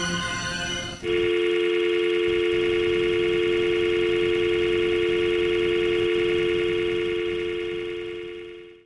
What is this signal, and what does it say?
Dial tone (North America), 350 + 440 Hz.